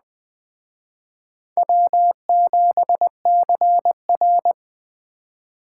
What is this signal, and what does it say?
Morse code, heard as a keyed tone, about 23 wpm, 700 Hz. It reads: W7CR
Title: Speed Racing - USA 1x2 Call Signs 20wpm